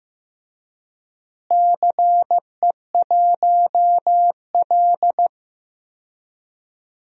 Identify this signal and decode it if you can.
Morse code, heard as a keyed tone, about 15 wpm, 700 Hz. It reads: CE1L
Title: DXCC-100 Call Signs - 4 Characters - Repeated 3x - 15wpm